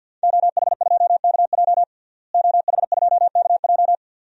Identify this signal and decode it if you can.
Morse code, heard as a keyed tone, about 52 wpm, 700 Hz. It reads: OH2KJ OH2KJ
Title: DXCC-100 Call Signs - 5 Characters - Repeated 2x - 50wpm